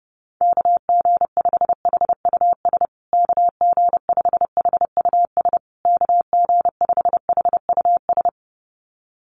Morse code, heard as a keyed tone, about 30 wpm, 700 Hz. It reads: KG5HUS KG5HUS KG5HUS
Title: USA Call Signs Repeated 3x 30wpm